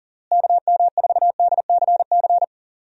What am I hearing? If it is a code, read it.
Morse code, heard as a keyed tone, about 40 wpm, 700 Hz. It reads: KM4DCC
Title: USA Call Signs - 6 Characters - Repeated 3x - 40wpm